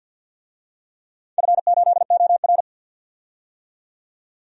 Morse code, heard as a keyed tone, about 53 wpm, 700 Hz. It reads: F8OR
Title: DXCC-100 Call Signs - 4 Characters - 50wpm